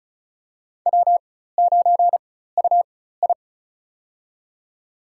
Morse code, heard as a keyed tone, about 36 wpm, spaced as for 16 wpm, 700 Hz. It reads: W9UI
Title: USA Call Signs - 4 Characters - 4x Character Spacing - 35wpm